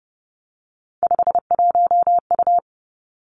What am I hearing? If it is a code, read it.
Morse code, heard as a keyed tone, about 30 wpm, 700 Hz. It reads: 51U
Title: Three Challenging Letter-Number Combinations 30wpm